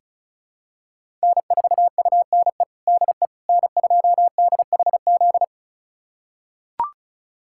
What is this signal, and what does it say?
Morse code, heard as a keyed tone, about 36 wpm, 700 Hz. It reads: N4UNE DE N2DHZ
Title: USA Callsign DE Callsign - 5 Characters - 35wpm